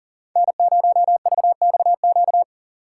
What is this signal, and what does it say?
Morse code, heard as a keyed tone, about 40 wpm, 700 Hz. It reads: N0VXQ
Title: USA Call Signs - 5 Characters - 4x Character Spacing - 40wpm